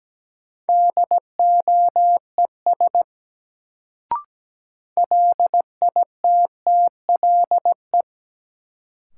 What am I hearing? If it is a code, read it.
Morse code, heard as a keyed tone, about 17 wpm, 700 Hz. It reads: DOES LITTLE
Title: Sets of 2 Words - New Word Drill - 17wpm